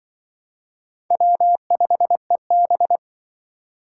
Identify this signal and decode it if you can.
Morse code, heard as a keyed tone, about 25 wpm, 700 Hz. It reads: W5EB
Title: Speed Racing - USA Call Signs 20wpm